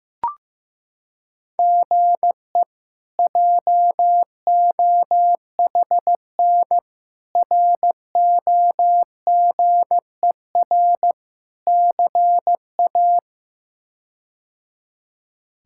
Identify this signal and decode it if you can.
Morse code, heard as a keyed tone, about 16 wpm, 700 Hz. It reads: GE JOHN ROGER CA
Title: K1USN SST QSO Practice - 15wpm